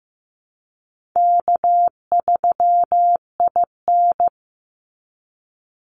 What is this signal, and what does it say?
Morse code, heard as a keyed tone, about 15 wpm, 700 Hz. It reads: K3IN
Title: Call Signs Repeated 3x - 15wpm